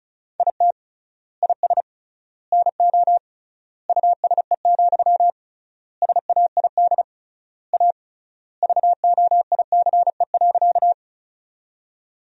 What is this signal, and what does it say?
Morse code, heard as a keyed tone, about 34 wpm, 700 Hz. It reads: IT IS NO USE, SAID A VOICE.
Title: 3x Word-Spacing - Sentences from Top 500 Words - 35wpm